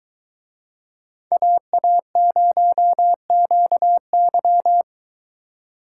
Morse code, heard as a keyed tone, about 21 wpm, 700 Hz. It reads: AA0QY